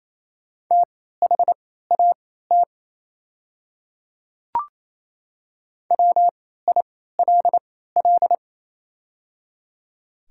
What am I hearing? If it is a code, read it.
Morse code, heard as a keyed tone, about 28 wpm, spaced as for 16 wpm, 700 Hz. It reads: THAT WILL
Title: 3x Character Spacing - Top 100 Words - 28wpm